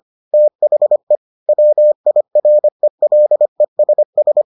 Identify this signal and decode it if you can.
Morse code, heard as a keyed tone, about 25 wpm, 600 Hz. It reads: THE WIRELESS